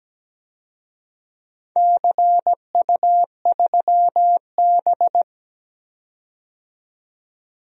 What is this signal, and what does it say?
Morse code, heard as a keyed tone, about 17 wpm, 700 Hz. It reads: CU3B
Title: DXCC-100 Call Signs - 4 Characters - 17wpm